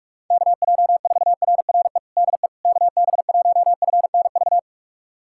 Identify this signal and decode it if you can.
Morse code, heard as a keyed tone, about 46 wpm, 690 Hz. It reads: KJ4RRE DE KB1FNV
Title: USA Callsign DE Callsign - 6 Characters - 45wpm